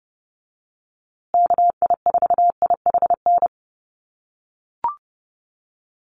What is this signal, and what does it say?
Morse code, heard as a keyed tone, about 30 wpm, 700 Hz. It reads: KI4IHN
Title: USA Call Signs Repeated 3x 30wpm